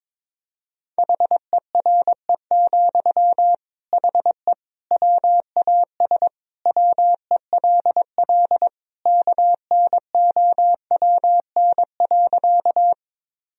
Morse code, heard as a keyed tone, about 22 wpm, 700 Hz. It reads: HERE, HE WAS WELL KNOWN.
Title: Sentences from Top 1000 Words - Review Part 3 - 22wpm